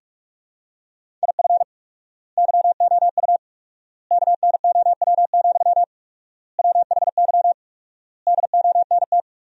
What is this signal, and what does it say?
Morse code, heard as a keyed tone, about 41 wpm, 700 Hz. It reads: IF YOU KNOW, WHY DONT
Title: 4x Word-Spacing - Sentences from Top 300 Words - 45wpm